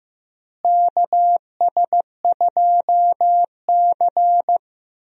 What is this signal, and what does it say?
Morse code, heard as a keyed tone, about 15 wpm, 700 Hz. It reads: KS2C
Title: USA Call Signs - 4 Characters - Repeated 2x - 15wpm